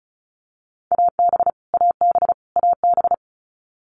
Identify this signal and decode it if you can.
Morse code, heard as a keyed tone, about 34 wpm, 700 Hz. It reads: AB AB AB
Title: Canadian Provinces and Territories - Repeated 3x - 35wpm